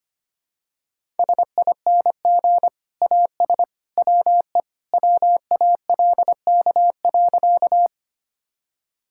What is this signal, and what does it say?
Morse code, heard as a keyed tone, about 25 wpm, 690 Hz. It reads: SING AS WE WALK.
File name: practice / 25wpm / Sentences with 2-Word Phrases - 25wpm